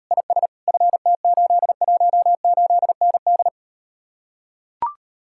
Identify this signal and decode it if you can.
Morse code, heard as a keyed tone, about 38 wpm, 700 Hz. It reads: IS FT818ND